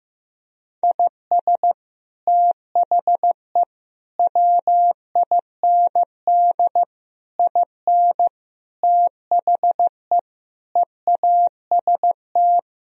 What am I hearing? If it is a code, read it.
Morse code, heard as a keyed tone, about 15 wpm, 700 Hz. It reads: IS THE WIND IN THE EAST
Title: Sentences from Top 1000 Words - Review Part 3 - 15wpm